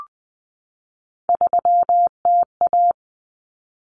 Morse code, heard as a keyed tone, about 20 wpm, 700 Hz. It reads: 3TA
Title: Common Cut Numbers - 3-Digit 20wpm